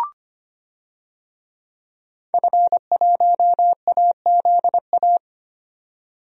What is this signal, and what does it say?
Morse code, heard as a keyed tone, about 25 wpm, 700 Hz. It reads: F1AZA
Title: DXCC-100 Call Signs - 5 Characters - 25wpm